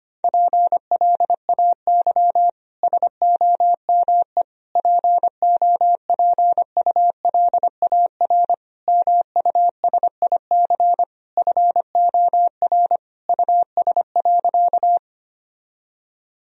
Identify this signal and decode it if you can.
Morse code, heard as a keyed tone, about 25 wpm, 700 Hz. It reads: PLAY SOME POPULAR MUSIC FOR US.
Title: Sentences from Top 1000 Words - Review Part 3 - 25wpm